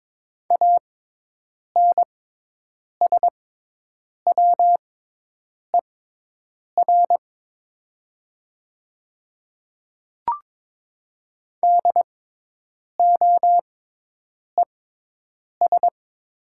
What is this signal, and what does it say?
Morse code, heard as a keyed tone, about 21 wpm, spaced as for 8 wpm, 700 Hz. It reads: ANSWER DOES
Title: 6x Character Spacing - Top 300 Words - 22wpm